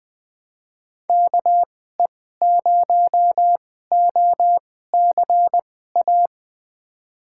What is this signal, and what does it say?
Morse code, heard as a keyed tone, about 20 wpm, spaced as for 14 wpm, 700 Hz. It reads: KE0OCA
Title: USA Call Signs - 6 Characters - 2x Character Spacing - 20wpm